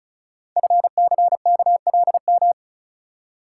Morse code, heard as a keyed tone, about 36 wpm, 700 Hz. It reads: FCKLM
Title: Code Groups - 5 Letters - 35wpm